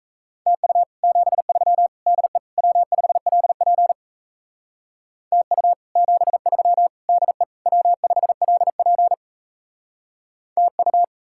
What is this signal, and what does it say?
Morse code, heard as a keyed tone, about 40 wpm, 700 Hz. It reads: TU 73 DE W5LP TU 73 DE W5LP TU